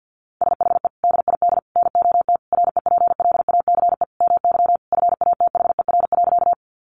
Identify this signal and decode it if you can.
Morse code, heard as a keyed tone, about 50 wpm, 700 Hz. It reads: SHE DID NOT REPLACE MY FATHER.
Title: Sentences from Top 1400 Words 50wpm